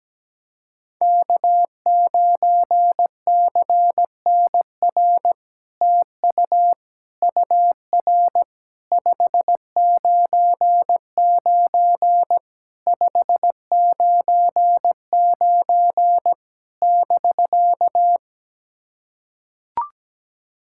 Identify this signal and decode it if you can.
Morse code, heard as a keyed tone, about 17 wpm, 700 Hz. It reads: K9CNR TU UR 599 599 <BK>
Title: SOTA QSO Practice 17wpm